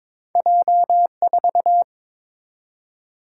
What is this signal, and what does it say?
Morse code, heard as a keyed tone, about 22 wpm, 690 Hz. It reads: J4
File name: practice / 22wpm / Two Letter-Number Combinations - 22wpm